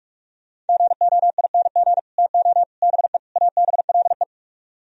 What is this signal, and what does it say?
Morse code, heard as a keyed tone, about 63 wpm, spaced as for 54 wpm, 700 Hz. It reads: GOING TO BE ABLE
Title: Speed-Racing - Sets of 4 Words 45wpm